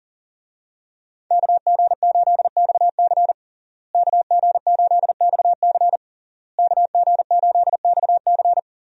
Morse code, heard as a keyed tone, about 37 wpm, 700 Hz. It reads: KG8XC KG8XC KG8XC
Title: USA Call Signs - 5 Characters - Repeated 3x - 40wpm